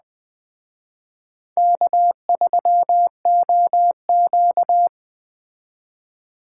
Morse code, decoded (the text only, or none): K3OQ